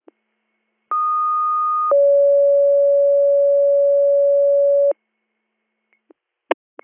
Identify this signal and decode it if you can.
Two-tone page, heard as a one-second tone, then a three-second tone, about 1230 then 560 Hz. It is heard from 0.9 to 4.9 s.